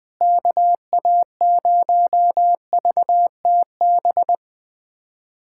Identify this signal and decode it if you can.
Morse code, heard as a keyed tone, about 20 wpm, 700 Hz. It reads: KA0VTB
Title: USA Callsign DE Callsign 20wpm